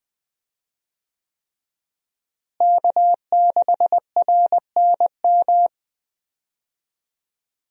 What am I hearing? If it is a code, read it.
Morse code, heard as a keyed tone, about 20 wpm, 700 Hz. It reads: K6RNM
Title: USA Call Signs - 5 Characters - Repeated 3x - 20wpm